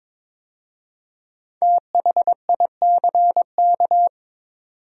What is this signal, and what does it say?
Morse code, heard as a keyed tone, about 22 wpm, 700 Hz. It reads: THICK
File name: 5 Letter Words - First 3 Letters Given - 22wpm